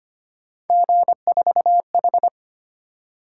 Morse code, heard as a keyed tone, about 25 wpm, 700 Hz. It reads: G4H
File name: Three Letter-Number Combinations - 25wpm